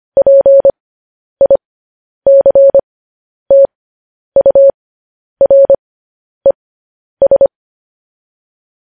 Morse code, heard as a keyed tone, about 25 wpm, spaced as for 10 wpm, 550 Hz. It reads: PICTURES
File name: podcast / News headlines-1x-25x10